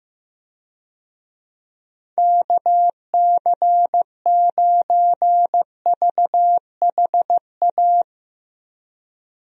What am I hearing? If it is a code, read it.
Morse code, heard as a keyed tone, about 15 wpm, 700 Hz. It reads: KC9VHA
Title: USA Call Signs - 6 Characters - Repeated 3x - 15wpm